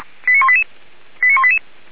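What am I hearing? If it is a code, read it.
Five-tone selective call (ZVEI), heard as quick runs of five tones, tones of about 70 ms. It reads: 89188 89188
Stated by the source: ZVEI five-tone call 89200 received by SDR, archived on the Feuerwehr Orlamünde fire brigade server.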